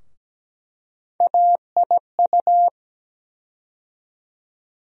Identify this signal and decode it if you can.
Morse code, heard as a keyed tone, about 17 wpm, 700 Hz. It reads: AIU
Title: USA Call Sign Postfixes - 3 Letters 17wpm